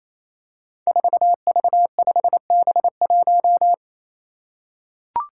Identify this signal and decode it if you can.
Morse code, heard as a keyed tone, about 28 wpm, 700 Hz. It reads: 4V5B1
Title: Five Challenging Letter-Number Combinations 28wpm